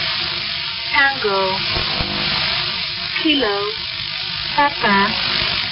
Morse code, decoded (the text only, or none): none